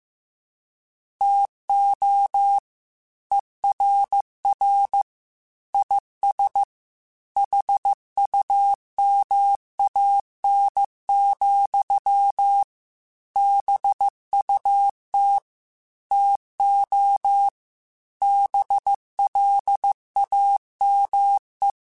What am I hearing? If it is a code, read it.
Morse code, heard as a keyed tone, about 15 wpm, 800 Hz. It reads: TO ERR IS HUMAN, BUT TO BLAME
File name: Chuckles / Chuckles-mp3s / HumanErr @13